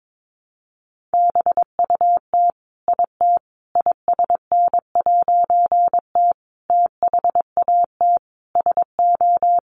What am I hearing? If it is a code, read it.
Morse code, heard as a keyed tone, about 22 wpm, 700 Hz. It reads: BUT IT ISN'T THAT SO
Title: A Sound Of Thunder 22wpm